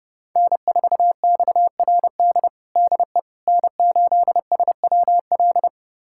Morse code, heard as a keyed tone, about 31 wpm, 700 Hz. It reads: N4XRD DE N8SWL
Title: USA Callsign DE Callsign 30wpm